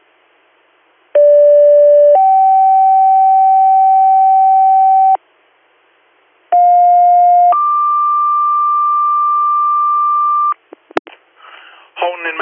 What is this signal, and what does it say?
Two-tone page, heard as a one-second tone, then a three-second tone, about 580 then 770 Hz, about 710 then 1180 Hz, with speech. It is heard from 1.1 to 10.5 s.